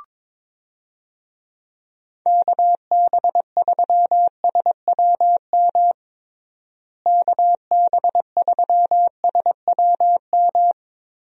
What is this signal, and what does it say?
Morse code, heard as a keyed tone, about 21 wpm, 700 Hz. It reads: KB3SWM KB3SWM
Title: USA Call Signs - 6 Characters - Repeated 3x - 22wpm